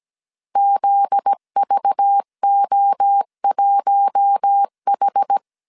Morse code, heard as a keyed tone, about 17 wpm, 800 Hz. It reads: ZVO1H
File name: CW / DET / 1 9 1500C S C 16wpm 2909